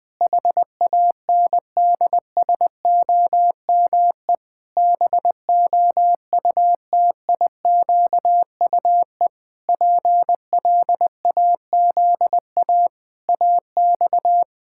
Morse code, heard as a keyed tone, about 20 wpm, 700 Hz. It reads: HANDSOME BOUTIQUE PLAZA AX
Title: Pangram Challenge v2 - 20wpm